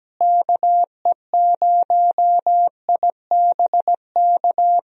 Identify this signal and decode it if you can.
Morse code, heard as a keyed tone, about 17 wpm, 700 Hz. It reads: KE0IBK
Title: USA Call Signs - 6 Characters - Repeated 2x - 17wpm